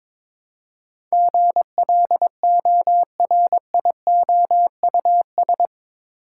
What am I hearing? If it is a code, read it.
Morse code, heard as a keyed tone, about 22 wpm, 700 Hz. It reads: GLORIOUS
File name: practice / 22wpm / Eight Letter Words - 22wpm